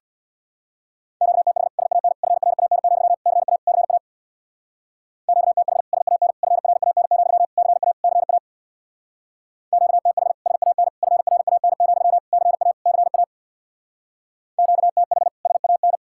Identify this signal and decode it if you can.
Morse code, heard as a keyed tone, about 63 wpm, 700 Hz. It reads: QTH SAN FRAN, CA CA QTH SAN FRAN, CA CA QTH SAN FRAN, CA CA QTH SAN